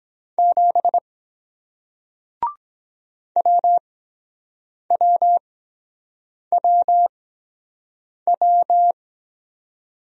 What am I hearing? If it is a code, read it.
Morse code, heard as a keyed tone, about 23 wpm, spaced as for 7 wpm, 700 Hz. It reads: 7 WWWW